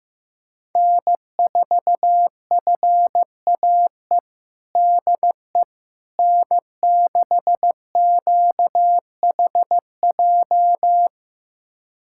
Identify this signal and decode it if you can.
Morse code, heard as a keyed tone, about 15 wpm, 700 Hz. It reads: N4FAE DE N6QHJ
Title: USA Callsign DE Callsign - 5 Characters - 15wpm